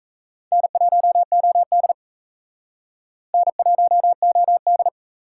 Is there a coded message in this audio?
Yes: Morse code from 0.5 to 4.9 s, heard as a keyed tone, about 37 wpm, 700 Hz. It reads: N1OD N1OD